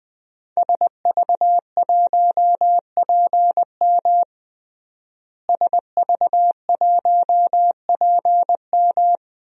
Morse code, heard as a keyed tone, about 21 wpm, 700 Hz. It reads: SV1PM SV1PM
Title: DXCC-100 Call Signs - 5 Characters - Repeated 3x - 20wpm